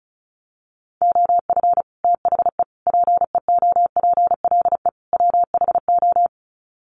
Morse code, heard as a keyed tone, about 34 wpm, 700 Hz. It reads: OF THE PEOPLE WHO
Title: Sets of 4 Words 35wpm